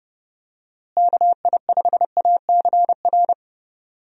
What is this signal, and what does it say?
Morse code, heard as a keyed tone, about 31 wpm, 700 Hz. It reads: KI5ACR
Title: USA Call Signs - 6 Characters - Repeated 2x - 30wpm